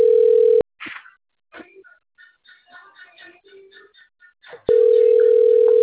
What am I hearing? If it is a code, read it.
Ringback tone (North America), 440 + 480 Hz.